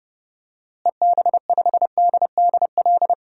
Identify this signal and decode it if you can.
Morse code, heard as a keyed tone, about 29 wpm, 700 Hz. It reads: EB5DDL
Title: DXCC-100 Call Signs - 6 Characters - Repeated 2x - 30wpm